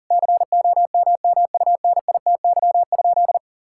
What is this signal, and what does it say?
Morse code, heard as a keyed tone, about 37 wpm, 700 Hz. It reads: COMMUNITY?
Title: Sentences from Top 1000 Words - Review Part 5 - 40wpm